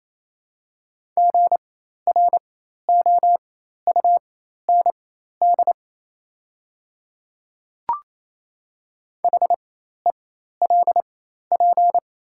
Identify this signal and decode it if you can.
Morse code, heard as a keyed tone, about 28 wpm, spaced as for 13 wpm, 690 Hz. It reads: GROUND HELP
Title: 4x Character Spacing - Top 500 Words - 28wpm